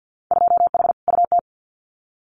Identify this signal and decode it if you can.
Morse code, heard as a keyed tone, about 50 wpm, 700 Hz. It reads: 2H UT
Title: Winter Field Day Exchanges Encore 50wpm